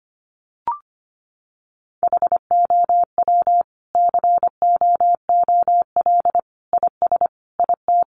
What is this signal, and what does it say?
Morse code, heard as a keyed tone, about 25 wpm, 700 Hz. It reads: HOW COOL IS IT
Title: Sentences from Top 1200 Words 25wpm